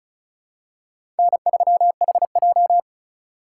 Morse code, heard as a keyed tone, about 36 wpm, 700 Hz. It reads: N3HJ
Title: USA Call Signs - 4x Character Spacing - 35wpm